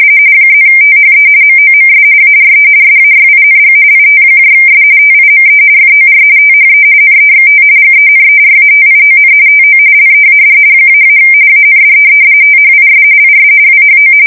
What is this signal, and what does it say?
RTTY, about 45.45 baud. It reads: +THE MOMS ARE SAYING ITS TIME FOR THE NEXT GENERATION TO LEAD -- AND THIS IS WHO
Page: 170 Shift 2125/2295